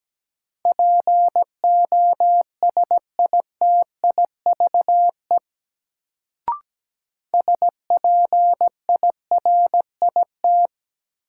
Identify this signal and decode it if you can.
Morse code, heard as a keyed tone, about 17 wpm, 700 Hz. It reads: POSITIVE SPIRIT
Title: Sentences from Top 1400 Words - New Word Drill Alt - 17wpm